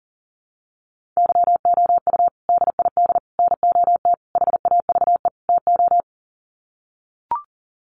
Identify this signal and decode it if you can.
Morse code, heard as a keyed tone, about 40 wpm, 700 Hz. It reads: YOU DID NOT HAVE TO
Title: Sets of 5 Words Encore 40wpm